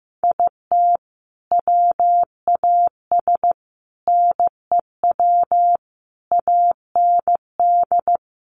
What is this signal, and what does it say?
Morse code, heard as a keyed tone, about 15 wpm, 700 Hz. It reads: IT WAS NEW AND
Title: Sentences from Top 1100 Words 15wpm